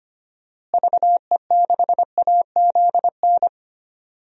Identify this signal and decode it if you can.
Morse code, heard as a keyed tone, about 25 wpm, 700 Hz. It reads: VE6AZN
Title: DXCC-100 Call Signs - 6 Characters - Repeated 3x - 25wpm